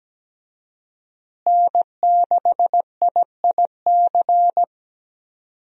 Morse code, heard as a keyed tone, about 17 wpm, 700 Hz. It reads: N6IIC
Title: USA Call Signs - 5 Characters - Repeated 3x - 17wpm